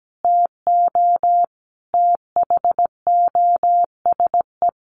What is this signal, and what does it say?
Morse code, heard as a keyed tone, about 17 wpm, 700 Hz. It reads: TO THOSE
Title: Sentences from Top 1500 Words 17wpm